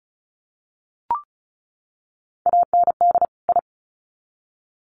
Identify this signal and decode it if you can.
Morse code, heard as a keyed tone, about 34 wpm, 700 Hz. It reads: AND I
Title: Sets of 2 Words 35wpm